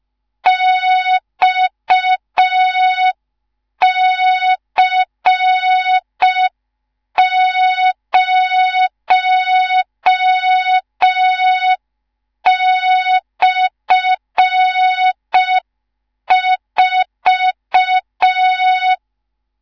Morse code, heard as a keyed tone, about 5 wpm, 3700 Hz. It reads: XC0/4
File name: Morze-HA2MN / hang / 19 GYAKORLAT